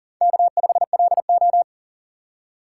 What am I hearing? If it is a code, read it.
Morse code, heard as a keyed tone, about 40 wpm, 700 Hz. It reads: K5LO
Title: USA Call Signs - 2x Character Spacing - 40wpm